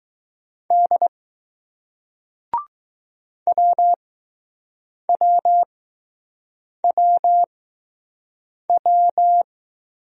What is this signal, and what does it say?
Morse code, heard as a keyed tone, about 21 wpm, spaced as for 6 wpm, 700 Hz. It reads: D WWWW